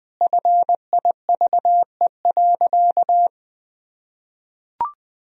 Morse code, heard as a keyed tone, about 20 wpm, 700 Hz. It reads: FIVE.
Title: Sentences from Top 1000 Words - Review Part 2 - 20wpm